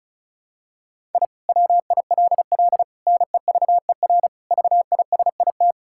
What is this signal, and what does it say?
Morse code, heard as a keyed tone, about 36 wpm, 700 Hz. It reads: I WILL NEVER VISIT
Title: Sentences from Top 1000 Words - Review Part 6 - 35wpm